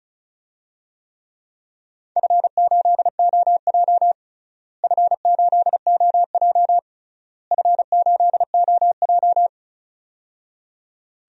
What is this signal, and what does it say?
Morse code, heard as a keyed tone, about 36 wpm, 700 Hz. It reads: F8OJ F8OJ F8OJ
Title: DXCC-100 Call Signs - 4 Characters - Repeated 3x - 35wpm